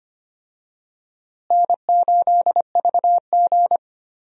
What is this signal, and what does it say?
Morse code, heard as a keyed tone, about 25 wpm, 700 Hz. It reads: N8VG
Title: USA Call Signs - 4 Characters - Repeated 2x - 25wpm